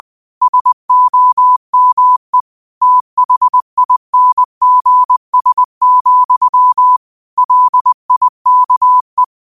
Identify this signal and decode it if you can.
Morse code, heard as a keyed tone, about 20 wpm, 1000 Hz. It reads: SOME THINGS, LIKE